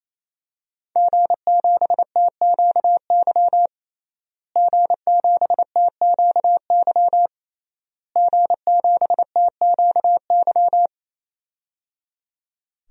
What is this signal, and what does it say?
Morse code, heard as a keyed tone, about 28 wpm, 700 Hz. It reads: G7TQY G7TQY G7TQY
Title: DXCC-100 Call Signs - 5 Characters - Repeated 3x - 28wpm